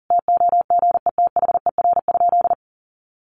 Morse code, heard as a keyed tone, about 40 wpm, 700 Hz. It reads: TOGETHER?
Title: Sentences from Top 1100 Words 40wpm